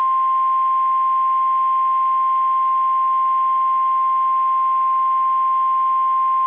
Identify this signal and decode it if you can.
Emergency Alert System (SAME) tones (the 1050 Hz attention tone). It is heard from 0.9 to 5.9 s.
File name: eas / KSDZKDJL / KSDZKDJL